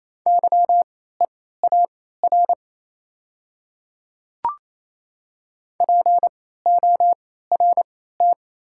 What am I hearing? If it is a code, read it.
Morse code, heard as a keyed tone, about 28 wpm, spaced as for 16 wpm, 690 Hz. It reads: YEAR PORT
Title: 3x Character Spacing - Top 200 Words - 28wpm